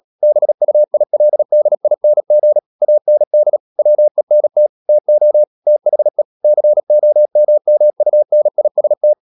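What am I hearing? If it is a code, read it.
Morse code, heard as a keyed tone, about 37 wpm, 600 Hz. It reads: BUILDING AND WENT TO THE COMMUNIST
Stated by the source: Morse code practice at 35–39 wpm sending English text of Ken Follett's novel Never, from CW med Gnister.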